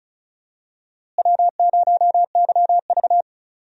Morse code, heard as a keyed tone, about 36 wpm, 700 Hz. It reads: W0YV
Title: USA Call Signs - 4 Characters - Repeated 2x - 35wpm